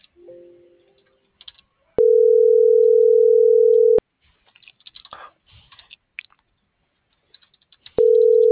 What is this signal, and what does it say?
Ringback tone (North America), 440 + 480 Hz.